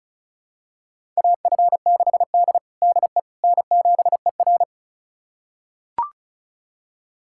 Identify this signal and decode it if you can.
Morse code, heard as a keyed tone, about 36 wpm, 700 Hz. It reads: AF6D DE N7ER
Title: USA Callsign DE Callsign - 4 Characters - 35wpm